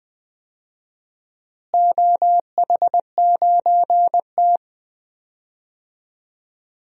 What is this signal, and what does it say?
Morse code, heard as a keyed tone, about 20 wpm, 700 Hz. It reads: OH9T
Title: DXCC-100 Call Signs - 4 Characters - Repeated 2x - 20wpm